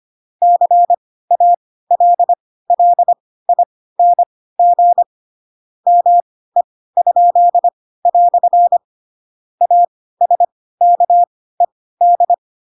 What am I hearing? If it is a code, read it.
Morse code, heard as a keyed tone, about 25 wpm, spaced as for 16 wpm, 700 Hz. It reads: CALLING ME?" ASKED
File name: Pinocchio / Pinocchio ch4 10 25 ik1hge